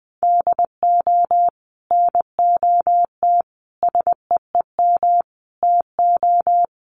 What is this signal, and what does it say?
Morse code, heard as a keyed tone, about 19 wpm, 700 Hz. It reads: DO NOT SEEM TO
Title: Sets of 5 Words Encore2 20wpm